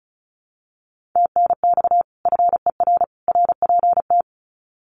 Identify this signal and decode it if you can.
Morse code, heard as a keyed tone, about 34 wpm, 700 Hz. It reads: TNX FER RPT
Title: Common QSO Phrases 35wpm